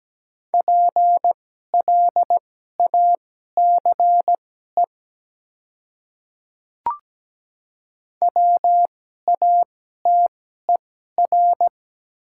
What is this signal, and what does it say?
Morse code, heard as a keyed tone, about 17 wpm, spaced as for 12 wpm, 700 Hz. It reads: PLACE WATER
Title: 2x Character Spacing - Top 200 Words - 17wpm